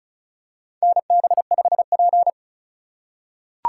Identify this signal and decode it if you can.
Morse code, heard as a keyed tone, about 36 wpm, 700 Hz. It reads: NB5P
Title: USA Call Signs - 4 Characters - 2x Character Spacing - 35wpm